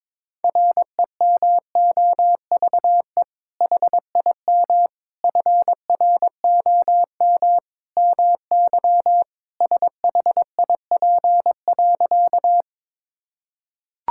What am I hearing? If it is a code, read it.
Morse code, heard as a keyed tone, about 22 wpm, 700 Hz. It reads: REMOVE HIM FROM MY SHIP.
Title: Sentences from Top 1000 Words - Review Part 5 - 22wpm